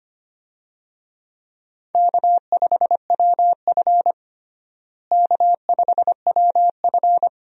Morse code, heard as a keyed tone, about 25 wpm, 700 Hz. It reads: K5WF K5WF
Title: USA Call Signs - 4 Characters - Repeated 3x - 25wpm